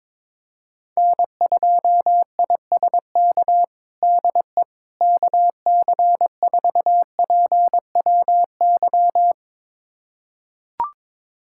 Morse code, heard as a keyed tone, about 22 wpm, 700 Hz. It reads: N2ISK DE KC4PWY
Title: USA Callsign DE Callsign 22wpm